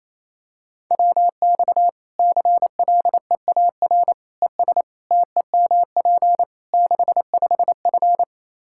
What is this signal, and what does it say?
Morse code, heard as a keyed tone, about 28 wpm, 700 Hz. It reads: WX CLEAR ES TEMP 65F
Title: Practice QSO - Exchange 2 - Info - 28wpm